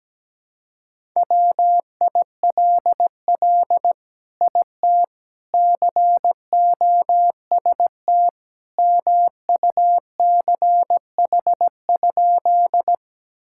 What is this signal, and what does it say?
Morse code, heard as a keyed tone, about 17 wpm, 700 Hz. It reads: WILL IT COST MUCH?
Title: Sentences from Top 1000 Words - Review Part 5 - 17wpm